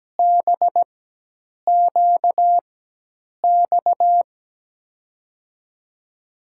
Morse code, heard as a keyed tone, about 17 wpm, spaced as for 8 wpm, 700 Hz. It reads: BQX
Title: USA Call Signs - 4x Character Spacing - 17wpm